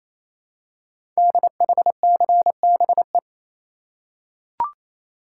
Morse code, heard as a keyed tone, about 27 wpm, 700 Hz. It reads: DHCBE
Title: Code Groups - 5 Letters - 28wpm